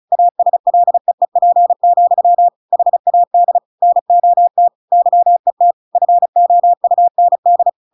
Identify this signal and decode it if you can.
Morse code, heard as a keyed tone, about 34 wpm, 700 Hz. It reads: ASLEEP, HAD NOT YET FOUND
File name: Pinocchio / Pinocchio ch7 35 35 ik1hge